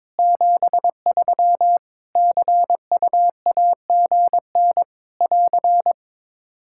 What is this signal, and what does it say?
Morse code, heard as a keyed tone, about 22 wpm, 700 Hz. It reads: 73 CUAGN <AR>
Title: Practice QSO - Exchange 4 - Ending - 22wpm